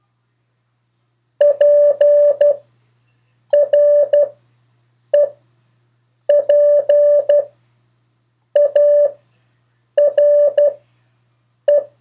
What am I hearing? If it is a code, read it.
Morse code, heard as a keyed tone, about 12 wpm, spaced as for 6 wpm, 580 Hz. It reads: PREPARE
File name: 4of4-sentences-mistakes-test